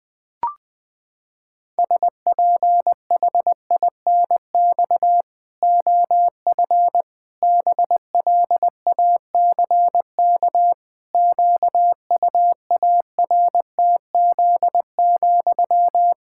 Morse code, heard as a keyed tone, about 20 wpm, 700 Hz. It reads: SPHINX OF BLACK QUARTZ,
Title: Pangram Challenge v2 - 20wpm